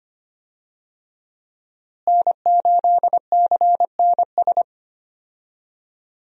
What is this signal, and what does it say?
Morse code, heard as a keyed tone, about 25 wpm, 700 Hz. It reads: N8CNS
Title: USA Call Signs - 5 Characters - Repeated 3x - 25wpm